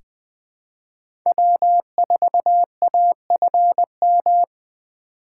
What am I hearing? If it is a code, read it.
Morse code, heard as a keyed tone, about 20 wpm, 700 Hz. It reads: W4AFM
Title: USA Call Signs - 5 Characters - 4x Character Spacing - 20wpm